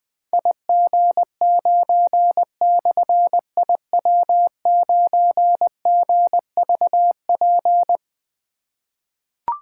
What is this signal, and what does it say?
Morse code, heard as a keyed tone, about 20 wpm, 700 Hz. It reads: IG9/IW9GVP
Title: Difficult Call Signs 20wpm